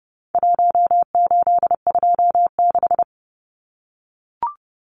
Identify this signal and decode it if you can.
Morse code, heard as a keyed tone, about 30 wpm, 700 Hz. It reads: 1826
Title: Numbers - Contest Style - 4-Digit 30wpm